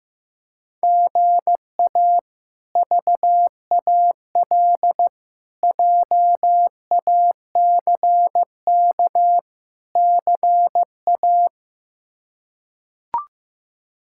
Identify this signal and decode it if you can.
Morse code, heard as a keyed tone, about 16 wpm, 700 Hz. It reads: GA VAL JACK CA
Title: K1USN SST QSO Practice - 15wpm